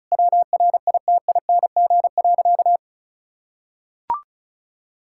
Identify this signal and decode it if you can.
Morse code, heard as a keyed tone, about 36 wpm, 700 Hz. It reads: WRITING.
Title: Sentences from Top 1600 Words 35wpm